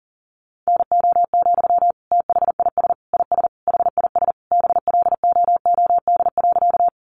Morse code, heard as a keyed tone, about 40 wpm, 700 Hz. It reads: NO, THIS IS HIS BLOOD.
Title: Sentences from Top 1100 Words 40wpm